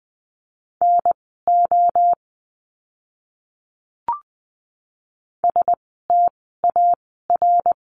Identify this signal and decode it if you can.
Morse code, heard as a keyed tone, about 20 wpm, spaced as for 15 wpm, 700 Hz. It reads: NO STAR
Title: 061 Lesson 11f - Letter h - Words 20wpm